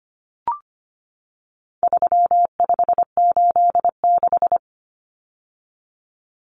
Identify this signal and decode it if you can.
Morse code, heard as a keyed tone, about 25 wpm, 700 Hz. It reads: 3586
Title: Numbers - 4-Digit 25wpm